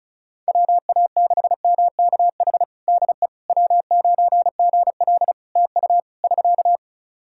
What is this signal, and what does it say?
Morse code, heard as a keyed tone, about 36 wpm, 700 Hz. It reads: WA6MKH DE W9GL TU <SK>
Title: Practice QSO - Exchange 4 - Ending - 35wpm